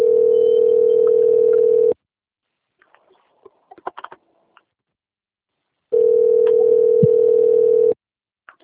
Ringback tone (North America), 440 + 480 Hz.